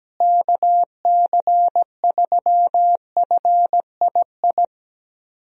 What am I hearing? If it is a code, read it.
Morse code, heard as a keyed tone, about 17 wpm, 700 Hz. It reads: KC3FII
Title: USA Call Signs - 6 Characters - Repeated 2x - 17wpm